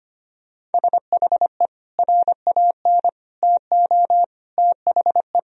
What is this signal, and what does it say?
Morse code, heard as a keyed tone, about 25 wpm, 700 Hz. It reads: SHE RAN TO THE
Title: Sentences from Top 1000 Words - Review Part 3 - 25wpm